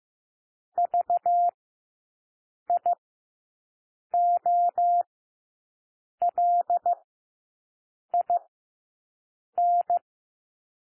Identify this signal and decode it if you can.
Morse code, heard as a keyed tone, about 15 wpm, spaced as for 6 wpm, 700 Hz. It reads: VIOLIN
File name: cwmp3 / W1 SOUND WORDS 15 wpm